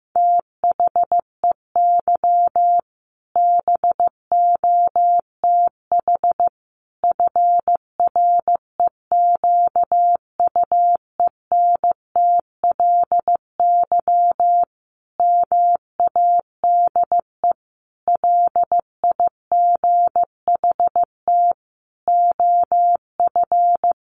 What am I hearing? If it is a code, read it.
Morse code, heard as a keyed tone, about 15 wpm, 700 Hz. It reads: THEY BOTH FREQUENTLY MADE LIGHT OF
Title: Sentences from Top 1300 Words 15wpm